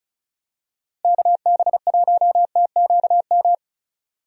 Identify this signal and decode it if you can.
Morse code, heard as a keyed tone, about 36 wpm, 700 Hz. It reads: KB1TQM
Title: USA Call Signs - 6 Characters - Repeated 3x - 35wpm